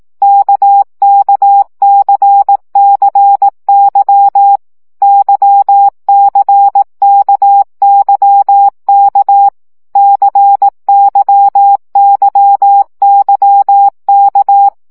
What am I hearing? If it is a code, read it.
Morse code, heard as a keyed tone, about 18 wpm, 800 Hz. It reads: KKCCY YCKYK CYYYK
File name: CW gyakorlas k c y 15 baud